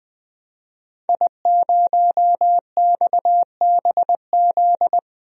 Morse code, heard as a keyed tone, about 20 wpm, 700 Hz. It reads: I0XBZ
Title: DXCC-100 Call Signs - 5 Characters - Repeated 2x - 20wpm